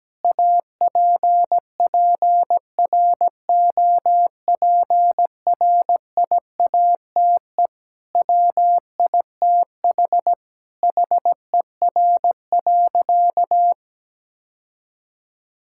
Morse code, heard as a keyed tone, about 17 wpm, 700 Hz. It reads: APPROPRIATE WITH HER.
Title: Sentences from Top 1000 Words - Review Part 5 - 17wpm